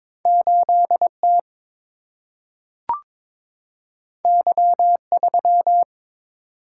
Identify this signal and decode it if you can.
Morse code, heard as a keyed tone, about 22 wpm, 700 Hz. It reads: 8T Y3
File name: practice / 22wpm / Two Letter-Number Combinations - 22wpm